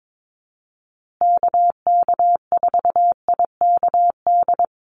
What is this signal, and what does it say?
Morse code, heard as a keyed tone, about 22 wpm, 700 Hz. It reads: KK4IKD
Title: USA Call Signs Repeated 2x 22wpm